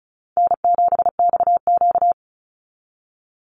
Morse code, heard as a keyed tone, about 34 wpm, 700 Hz. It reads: N7XQ
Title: USA Call Signs 35wpm